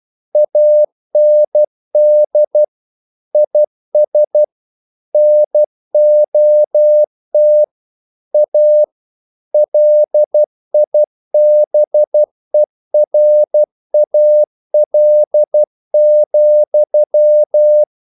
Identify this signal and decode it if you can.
Morse code, heard as a keyed tone, about 12 wpm, 600 Hz. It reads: AND IS NOT A LIBERAL,
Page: English quotations in CW (telegraphy - morse)